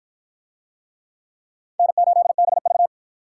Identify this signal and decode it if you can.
Morse code, heard as a keyed tone, about 50 wpm, 700 Hz. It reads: N8BV